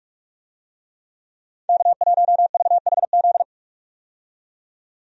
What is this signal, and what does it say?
Morse code, heard as a keyed tone, about 46 wpm, 700 Hz. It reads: K1VHZ
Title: USA Call Signs - 5 Characters - 45wpm